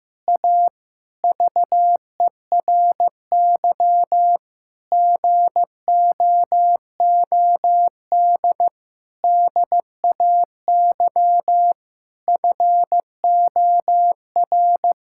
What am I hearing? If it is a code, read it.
Morse code, heard as a keyed tone, about 15 wpm, 700 Hz. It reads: A VERY GOOD DAY FOR
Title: Sentences from Top 1000 Words - Review Part 6 - 15wpm